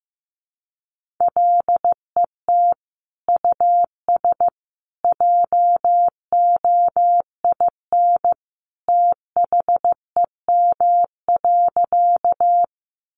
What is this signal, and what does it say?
Morse code, heard as a keyed tone, about 15 wpm, 700 Hz. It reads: LET US JOIN THEM.
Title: Sentences from Top 1400 Words 15wpm